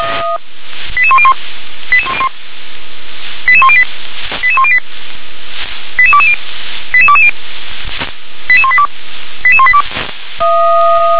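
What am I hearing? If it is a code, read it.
Five-tone selective call (ZVEI), heard as quick runs of five tones, tones of about 70 ms. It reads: 88101 88108 88108 88220 88220 88182 88182
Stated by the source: ZVEI five-tone call received on SDR.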